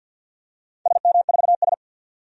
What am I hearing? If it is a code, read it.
Morse code, heard as a keyed tone, about 46 wpm, 700 Hz. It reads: SM4S
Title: DXCC-100 Call Signs - 4 Characters - 50wpm